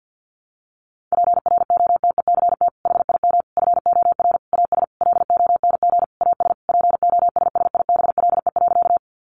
Morse code, heard as a keyed tone, about 50 wpm, 700 Hz. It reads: PROTECT HIM FOR AS LONG AS POSSIBLE.
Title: Sentences from Top 900 Words 50wpm